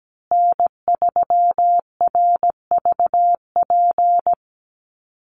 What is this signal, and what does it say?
Morse code, heard as a keyed tone, about 17 wpm, 700 Hz. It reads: N3RVP
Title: USA Call Signs Repeated 2x 17wpm